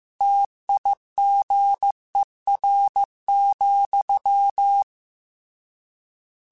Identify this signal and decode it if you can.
Morse code, heard as a keyed tone, about 15 wpm, 800 Hz. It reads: TIGER,
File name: Misc / Misc-mp3s / Word Pairs @13